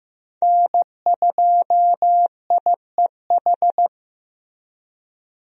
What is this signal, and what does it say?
Morse code, heard as a keyed tone, about 15 wpm, 700 Hz. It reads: N2IEH
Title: USA Call Signs - 5 Characters - Repeated 3x - 15wpm